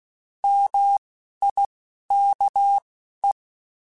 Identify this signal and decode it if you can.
Morse code, heard as a keyed tone, about 16 wpm, spaced as for 11 wpm, 800 Hz. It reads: MIKE
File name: Alphabetical / Alphabetical mp3s / Ms @10